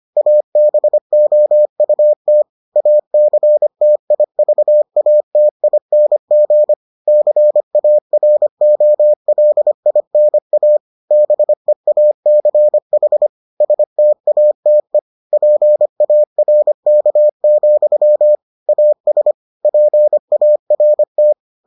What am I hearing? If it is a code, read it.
Morse code, heard as a keyed tone, about 25 wpm, 600 Hz. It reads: ABOUT ACTIVATING CAROLINA BEACH STATE PARK, AS PART